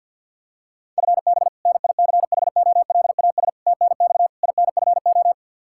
Morse code, heard as a keyed tone, about 52 wpm, 700 Hz. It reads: FB NICHOLAS TNX INFO
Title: K1USN SST QSO Practice 50wpm